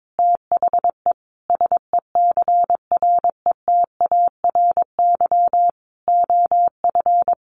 Morse code, heard as a keyed tone, about 22 wpm, 700 Hz. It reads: THE SECRETARY OF
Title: Sentences from Top 1300 Words 22wpm